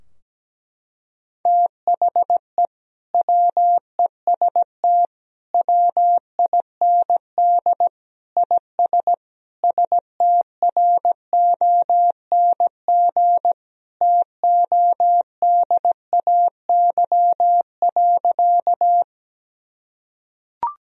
Morse code, heard as a keyed tone, about 17 wpm, 700 Hz. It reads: THE WEST WIND IS STRONG TODAY.
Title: Sentences from Top 1000 Words - Review Part 2 - 17wpm